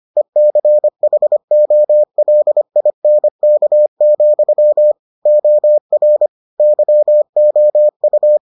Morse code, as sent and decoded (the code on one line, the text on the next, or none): . -.-. .... --- .-.. .. -. -.- --..-- / --- .-. / -.-- --- ..-
ECHOLINK, OR YOU